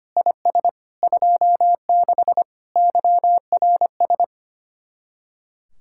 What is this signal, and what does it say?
Morse code, heard as a keyed tone, about 32 wpm, 700 Hz. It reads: IS 26 YRS